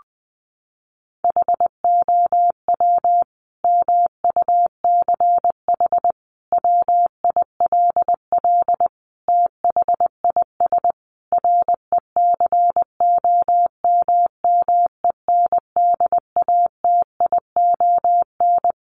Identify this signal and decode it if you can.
Morse code, heard as a keyed tone, about 20 wpm, 700 Hz. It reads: HOW MUCH WILL THIS RECOMMENDATION
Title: Sentences from Top 1400 Words 20wpm